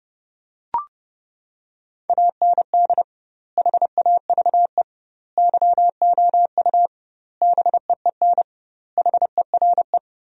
Morse code, heard as a keyed tone, about 30 wpm, 700 Hz. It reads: AND HAVE YOU BEEN HERE
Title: 2x Word-Spacing - Sentences from Top 200 Words - 30wpm